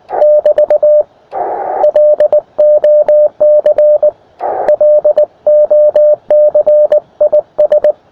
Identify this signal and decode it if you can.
Morse code, heard as a keyed tone, about 19 wpm, 600 Hz. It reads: <BT> LOC LOC IS
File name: attachments / europe 20m